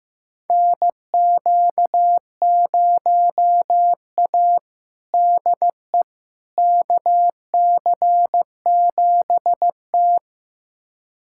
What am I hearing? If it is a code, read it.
Morse code, heard as a keyed tone, about 15 wpm, 700 Hz. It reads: NQ0A DE KC7T
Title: USA Callsign DE Callsign - 4 Characters - 15wpm